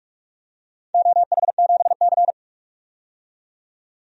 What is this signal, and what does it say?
Morse code, heard as a keyed tone, about 47 wpm, 700 Hz. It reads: OH7C
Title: DXCC-100 Call Signs - 4 Characters - 45wpm